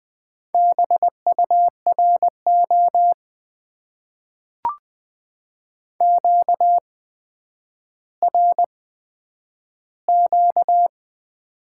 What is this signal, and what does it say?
Morse code, heard as a keyed tone, about 20 wpm, spaced as for 10 wpm, 700 Hz. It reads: BURO QRQ
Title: Top 90 QSO Elements - 8x Character Spacing - 20wpm